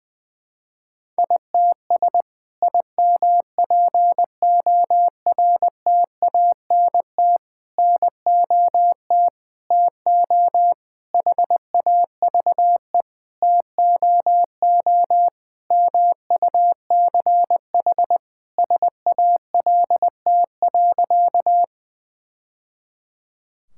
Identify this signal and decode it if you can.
Morse code, heard as a keyed tone, about 19 wpm, 700 Hz. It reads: ITS IMPORTANT NOT TO HAVE TOO MUCH SALT.
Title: Sentences from Top 1000 Words - Review Part 5 - 20wpm